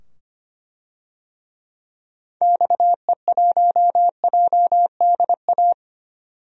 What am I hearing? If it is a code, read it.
Morse code, heard as a keyed tone, about 25 wpm, 700 Hz. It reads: XE1JDA
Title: DXCC-100 Call Signs - 6 Characters - 25wpm